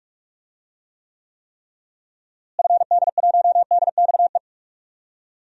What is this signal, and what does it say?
Morse code, heard as a keyed tone, about 46 wpm, 700 Hz. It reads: FD1DXE
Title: DXCC-100 Call Signs - 6 Characters - Repeated 3x - 45wpm